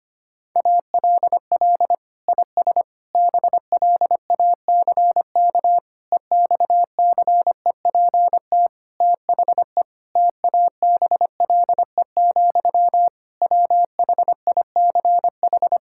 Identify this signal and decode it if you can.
Morse code, heard as a keyed tone, about 25 wpm, 700 Hz. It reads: ALL IS BLACK EXCEPT THE TABLE, WHICH
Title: Sentences from Top 1600 Words 25wpm